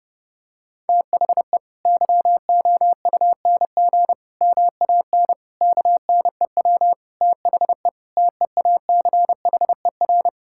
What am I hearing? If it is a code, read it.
Morse code, heard as a keyed tone, about 30 wpm, 700 Hz. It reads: THE YOUNG MAN KNEW THE TEACHER
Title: Sentences from Top 1000 Words - Review Part 6 - 30wpm